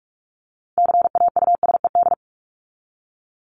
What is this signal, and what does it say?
Morse code, heard as a keyed tone, about 44 wpm, 700 Hz. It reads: CAUSED
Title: ED Suffix - Top 100 Words with Base Words 45wpm